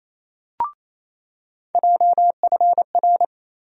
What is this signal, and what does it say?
Morse code, heard as a keyed tone, about 28 wpm, 700 Hz. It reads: JFR
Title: USA Call Sign Postfixes - 3 Letters 28wpm